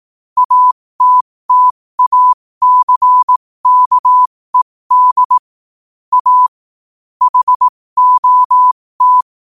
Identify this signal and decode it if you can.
Morse code, heard as a keyed tone, about 18 wpm, spaced as for 15 wpm, 1000 Hz. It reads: ATTACKED A HOT
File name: qotdfiles / 15wpm